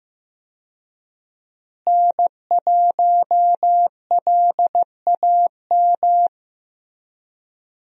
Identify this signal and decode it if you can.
Morse code, heard as a keyed tone, about 15 wpm, 700 Hz. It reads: N1LAM
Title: USA Call Signs - 5 Characters - Repeated 3x - 15wpm